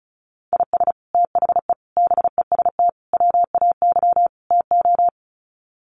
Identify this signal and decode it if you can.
Morse code, heard as a keyed tone, about 34 wpm, 700 Hz. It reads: IS THE BEST WAY TO
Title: Sets of 5 Words Encore 35wpm